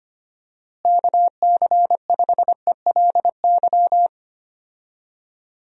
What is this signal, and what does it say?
Morse code, heard as a keyed tone, about 25 wpm, 700 Hz. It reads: KC5ELY
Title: USA Call Signs - 6 Characters - Repeated 2x - 25wpm